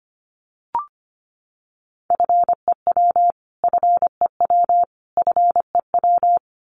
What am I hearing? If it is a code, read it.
Morse code, heard as a keyed tone, about 25 wpm, 700 Hz. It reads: FEW FEW FEW
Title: Top 300 Words Repeated 25wpm